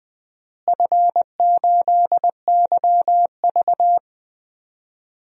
Morse code, heard as a keyed tone, about 20 wpm, 700 Hz. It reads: F8YV
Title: DXCC-100 Call Signs - 4 Characters - Repeated 3x - 20wpm